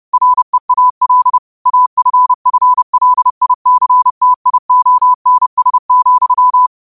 Morse code, heard as a keyed tone, about 30 wpm, 1000 Hz. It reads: REAL AFFLICTIONS,
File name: qotdfiles / 30wpm